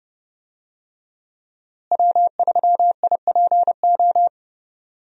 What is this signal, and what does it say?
Morse code, heard as a keyed tone, about 30 wpm, 700 Hz. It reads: W3IPO
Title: USA Call Signs - 5 Characters - Repeated 3x - 30wpm